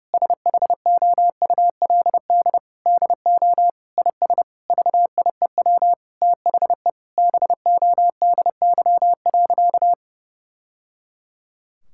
Morse code, heard as a keyed tone, about 30 wpm, 700 Hz. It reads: SHOULD DO IS VIEW THE BODY.
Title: Sentences from Top 1000 Words - Review Part 6 - 30wpm